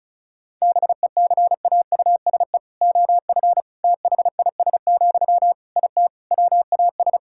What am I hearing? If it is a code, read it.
Morse code, heard as a keyed tone, about 34 wpm, 700 Hz. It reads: BECAUSE OF THIS, IT WAS
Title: Sentences from Top 1000 Words - Review Part 6 - 35wpm